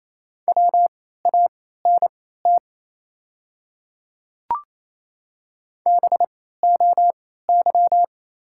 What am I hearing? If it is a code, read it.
Morse code, heard as a keyed tone, about 28 wpm, spaced as for 16 wpm, 700 Hz. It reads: WANT BOY
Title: 3x Character Spacing - Top 300 Words - 28wpm